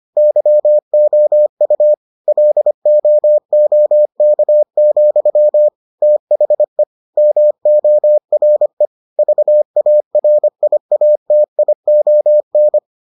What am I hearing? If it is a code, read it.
Morse code, heard as a keyed tone, about 25 wpm, 600 Hz. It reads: YOU LOOK, THE MORE VARIATION